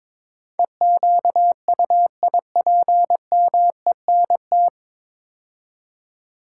Morse code, heard as a keyed tone, about 22 wpm, 700 Hz. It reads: EQUIPMENT
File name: 9 Letter Words - First 3 Letters Given - 22wpm